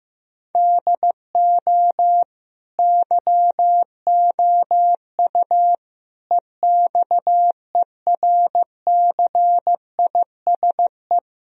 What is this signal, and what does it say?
Morse code, heard as a keyed tone, about 15 wpm, 700 Hz. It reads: DO YOU EXERCISE
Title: Sentences from Top 1000 Words - Review Part 4 - 15wpm